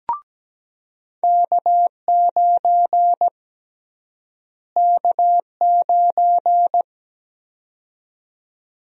Morse code, heard as a keyed tone, about 17 wpm, 700 Hz. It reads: K9 K9
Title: USA Call Sign Prefixes - Repeated 2x - 17wpm